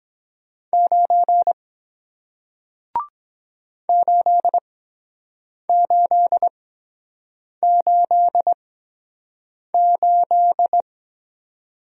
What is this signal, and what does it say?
Morse code, heard as a keyed tone, about 23 wpm, spaced as for 7 wpm, 700 Hz. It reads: 9 8888